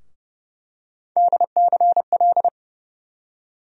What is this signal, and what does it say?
Morse code, heard as a keyed tone, about 30 wpm, 700 Hz. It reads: DCL
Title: USA Call Sign Postfixes - 3 Letters 30wpm